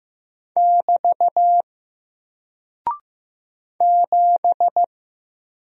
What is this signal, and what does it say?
Morse code, heard as a keyed tone, about 15 wpm, 700 Hz. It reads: <BT> 7
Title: Single Character (Letter:Number:Punctuation:Prosign) Alt 15wpm